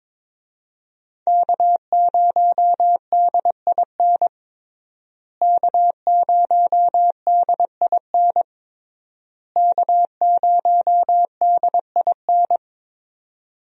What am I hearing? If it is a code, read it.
Morse code, heard as a keyed tone, about 22 wpm, 700 Hz. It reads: K0DIN K0DIN K0DIN
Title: USA Call Signs - 5 Characters - Repeated 3x - 22wpm